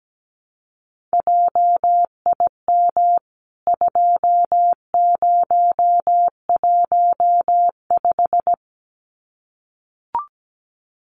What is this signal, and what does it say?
Morse code, heard as a keyed tone, about 17 wpm, 700 Hz. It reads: JIM 2015
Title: CWT Exchange 17wpm